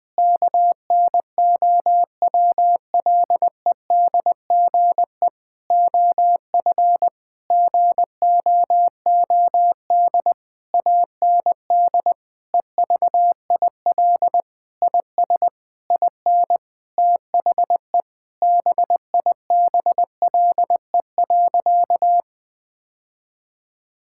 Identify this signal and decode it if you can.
Morse code, heard as a keyed tone, about 20 wpm, 700 Hz. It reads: KNOWLEDGE OF GOOD AND EVIL IS IN THE BIBLE.
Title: Sentences with 9-Word Phrases - 20wpm